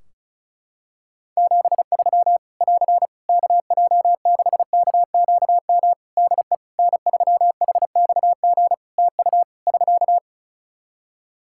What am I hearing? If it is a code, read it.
Morse code, heard as a keyed tone, about 36 wpm, 700 Hz. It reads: 73 <AR> KJ6KQM DE N3HXG TU <SK>
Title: Practice QSO - Exchange 4 - Ending - 35wpm